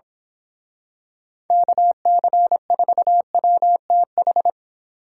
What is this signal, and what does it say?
Morse code, heard as a keyed tone, about 27 wpm, 700 Hz. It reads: KC4WTH
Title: Speed Racing - USA 2x3 Call Signs 22wpm